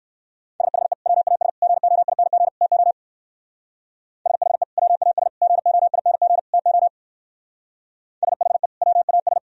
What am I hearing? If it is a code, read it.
Morse code, heard as a keyed tone, about 57 wpm, 700 Hz. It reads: SHE WAS GOING TO SHE WAS GOING TO SHE WAS